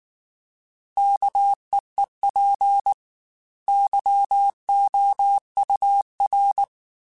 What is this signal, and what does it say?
Morse code, heard as a keyed tone, about 19 wpm, 800 Hz. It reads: KEEP YOUR
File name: Chuckles / Chuckles-mp3s / SoftWords @18